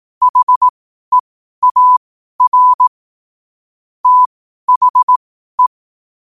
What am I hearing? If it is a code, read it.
Morse code, heard as a keyed tone, about 18 wpm, spaced as for 12 wpm, 1000 Hz. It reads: HEAR THE
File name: qotdfiles / 12wpm